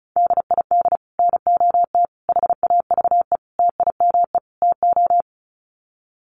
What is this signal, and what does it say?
Morse code, heard as a keyed tone, about 34 wpm, 700 Hz. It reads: DID NOT HAVE TIME TO
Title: Sets of 5 Words Encore 35wpm